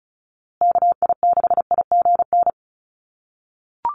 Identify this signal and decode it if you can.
Morse code, heard as a keyed tone, about 34 wpm, 700 Hz. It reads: KI6IGN
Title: USA Call Signs Repeated 2x 35wpm